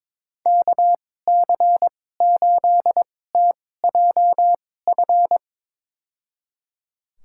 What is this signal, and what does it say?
Morse code, heard as a keyed tone, about 22 wpm, spaced as for 16 wpm, 700 Hz. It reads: KC8TJF
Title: USA Call Signs - 2x Character Spacing - 22wpm